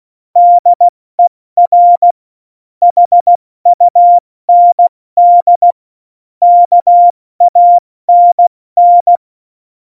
Morse code, heard as a keyed tone, about 16 wpm, spaced as for 14 wpm, 700 Hz. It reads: DER HUND KANN
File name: Morsekurs / Witze / Witz 06. Oktober 2020, 70 F80